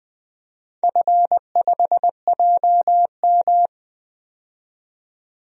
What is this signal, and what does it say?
Morse code, heard as a keyed tone, about 20 wpm, 700 Hz. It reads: F5JM
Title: DXCC-100 Call Signs - 4 Characters - 20wpm